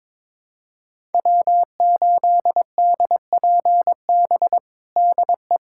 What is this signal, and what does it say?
Morse code, heard as a keyed tone, about 22 wpm, 700 Hz. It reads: W8DPB DE
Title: USA Callsign DE Callsign - 5 Characters - 22wpm